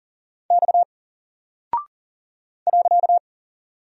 Morse code, heard as a keyed tone, about 39 wpm, 700 Hz. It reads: X .
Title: Single Character (Letter:Number:Punctuation:Prosign) Alt 40wpm